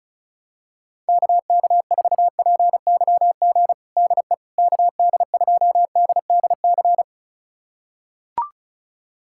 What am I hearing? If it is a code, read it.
Morse code, heard as a keyed tone, about 36 wpm, 700 Hz. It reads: KK4PYG DE KD2DDC
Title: USA Callsign DE Callsign 35wpm